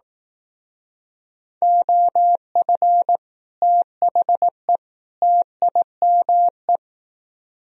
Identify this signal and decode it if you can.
Morse code, heard as a keyed tone, about 18 wpm, 700 Hz. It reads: OF THE TIME
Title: Speed-Racing - Sets of 3 Words 15wpm